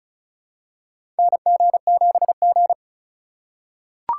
Morse code, heard as a keyed tone, about 36 wpm, 700 Hz. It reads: NG7G
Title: USA Call Signs - 4 Characters - 2x Character Spacing - 35wpm